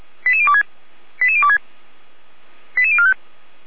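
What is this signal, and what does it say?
Five-tone selective call (ZVEI), heard as quick runs of five tones, tones of about 70 ms. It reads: 88026 88026 88045
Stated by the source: ZVEI five-tone call 88106 received on an SDR.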